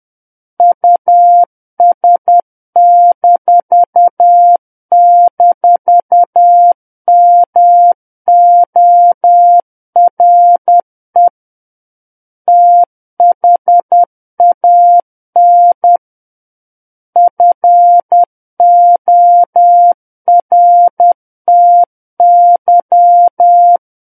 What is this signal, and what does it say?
Morse code, heard as a keyed tone, about 10 wpm, 700 Hz. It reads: US--MORE THAN FORTY
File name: Book One / 01-Chapter 1 - 10 WPM